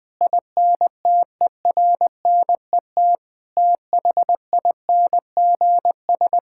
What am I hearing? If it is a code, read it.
Morse code, heard as a keyed tone, about 21 wpm, 700 Hz. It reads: INTERNET THINGS
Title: Sentences with 6-Word Phrases - 20wpm